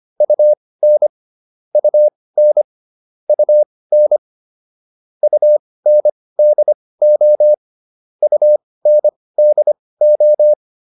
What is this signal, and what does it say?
Morse code, heard as a keyed tone, about 25 wpm, spaced as for 18 wpm, 600 Hz. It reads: UN UN UN UNDO UNDO